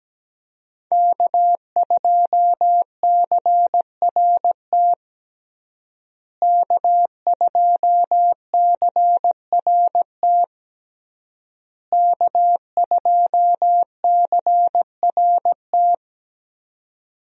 Morse code, heard as a keyed tone, about 17 wpm, 700 Hz. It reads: K2CRT K2CRT K2CRT
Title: USA Call Signs - 5 Characters - Repeated 3x - 17wpm